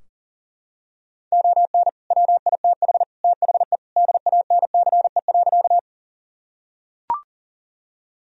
Morse code, heard as a keyed tone, about 40 wpm, 700 Hz. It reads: ON WITH THE DANCE.
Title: Sentences from Top 1000 Words - Review Part 4 - 40wpm